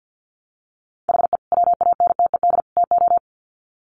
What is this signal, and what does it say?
Morse code, heard as a keyed tone, about 50 wpm, 700 Hz. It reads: HE WANTED TO
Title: Sets of 3 Words 50wpm